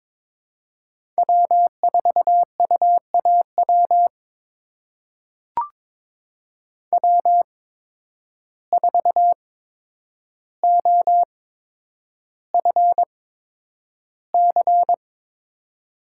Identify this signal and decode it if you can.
Morse code, heard as a keyed tone, about 22 wpm, spaced as for 10 wpm, 700 Hz. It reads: W4UAW W4OFC
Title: USA Call Signs - 8x Character Spacing - 22wpm